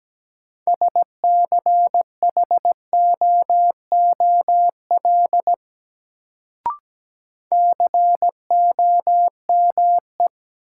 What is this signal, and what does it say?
Morse code, heard as a keyed tone, about 17 wpm, 700 Hz. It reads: SCHOOL COME
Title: Sets of 2 Words - New Word Drill - 17wpm